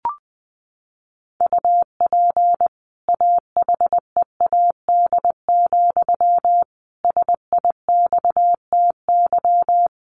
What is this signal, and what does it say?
Morse code, heard as a keyed tone, about 20 wpm, 700 Hz. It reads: UP AHEAD, SIXTY
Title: A Sound Of Thunder 20wpm